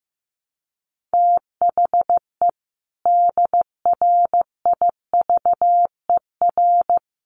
Morse code, heard as a keyed tone, about 15 wpm, 700 Hz. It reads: THE DRIVER
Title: Sentences from Top 1000 Words 15wpm